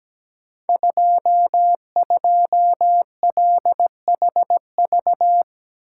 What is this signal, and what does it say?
Morse code, heard as a keyed tone, about 17 wpm, 700 Hz. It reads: 22LHV
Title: Code Groups - 5 Letter:Numbers - 17wpm